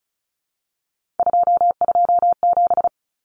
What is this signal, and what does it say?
Morse code, heard as a keyed tone, about 34 wpm, 700 Hz. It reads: 227
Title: Common Cut Numbers - 3-Digit 35wpm